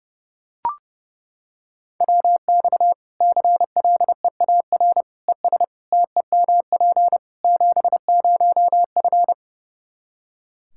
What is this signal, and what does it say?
Morse code, heard as a keyed tone, about 30 wpm, 700 Hz. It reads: WX CLEAR ES TEMP 70F
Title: K1USN SST QSO Practice 30wpm